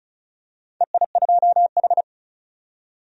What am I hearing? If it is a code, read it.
Morse code, heard as a keyed tone, about 34 wpm, 700 Hz. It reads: EI2H
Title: DXCC-100 Call Signs - 4 Characters - 35wpm